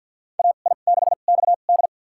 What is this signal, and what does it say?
Morse code, heard as a keyed tone, about 48 wpm, spaced as for 35 wpm, 700 Hz. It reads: AI6XD
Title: USA Call Signs - 5 Characters - 2x Character Spacing - 50wpm